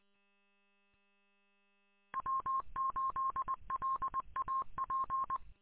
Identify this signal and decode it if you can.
Morse code, heard as a keyed tone, about 20 wpm, 1060 Hz. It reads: W8LAP
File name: w8lap- -combined